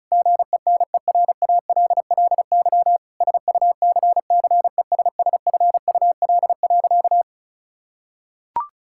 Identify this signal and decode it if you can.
Morse code, heard as a keyed tone, about 34 wpm, 700 Hz. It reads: GENERALLY SUCCESSFUL.
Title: Sentences from Top 1000 Words - Review Part 6 - 35wpm